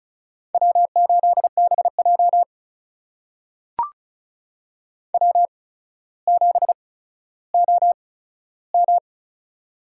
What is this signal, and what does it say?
Morse code, heard as a keyed tone, about 36 wpm, spaced as for 15 wpm, 700 Hz. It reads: W8BJ W7OM
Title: USA Call Signs - 4 Characters - 8x Character Spacing - 35wpm